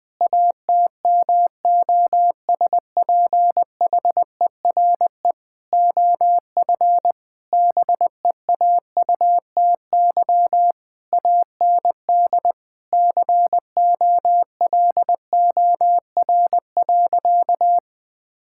Morse code, heard as a keyed tone, about 20 wpm, 700 Hz. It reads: ATMOSPHERE OF BEAUTY AND COLOR.
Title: Sentences from Top 1600 Words 20wpm